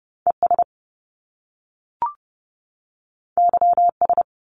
Morse code, heard as a keyed tone, about 30 wpm, 700 Hz. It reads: ES YS
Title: Top Two Letter Combinations in Words 30wpm